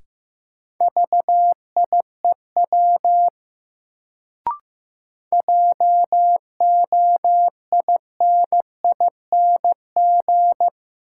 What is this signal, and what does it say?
Morse code, heard as a keyed tone, about 15 wpm, 700 Hz. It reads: VIEW JOINING
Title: Sets of 5 Words - New Word Drill - 15wpm